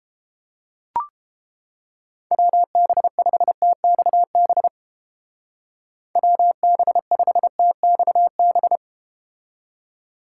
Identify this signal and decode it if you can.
Morse code, heard as a keyed tone, about 31 wpm, 700 Hz. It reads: WB5TXB WB5TXB